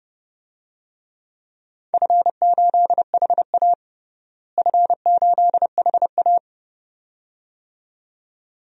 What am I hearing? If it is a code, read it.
Morse code, heard as a keyed tone, about 30 wpm, 700 Hz. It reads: F8HA F8HA
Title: DXCC-100 Call Signs - 4 Characters - Repeated 2x - 30wpm